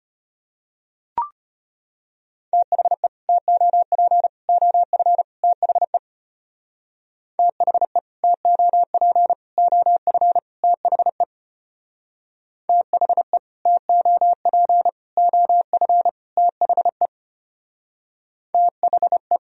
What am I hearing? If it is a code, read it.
Morse code, heard as a keyed tone, about 30 wpm, 700 Hz. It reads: THE TOP OF THE THE TOP OF THE THE TOP OF THE THE
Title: Speed-Racing - Sets of 4 Words 25wpm